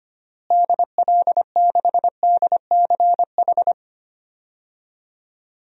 Morse code, heard as a keyed tone, about 25 wpm, 700 Hz. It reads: DL6DCH
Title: DXCC-100 Call Signs - 6 Characters - Repeated 3x - 25wpm